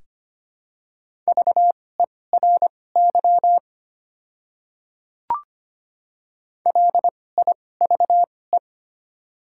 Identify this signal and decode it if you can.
Morse code, heard as a keyed tone, about 25 wpm, spaced as for 18 wpm, 700 Hz. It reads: VERY LIVE
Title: 2x Character Spacing - Top 200 Words - 25wpm